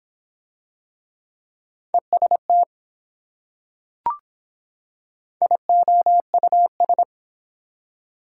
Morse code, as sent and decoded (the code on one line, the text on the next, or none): . ... - / .. --- ..- ...
EST IOUS